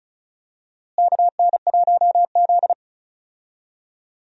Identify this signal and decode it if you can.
Morse code, heard as a keyed tone, about 34 wpm, 700 Hz. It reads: KN1Z
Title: USA Call Signs - 4 Characters - Repeated 3x - 35wpm